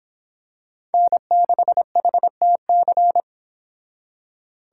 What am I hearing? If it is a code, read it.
Morse code, heard as a keyed tone, about 26 wpm, 700 Hz. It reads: N6HTC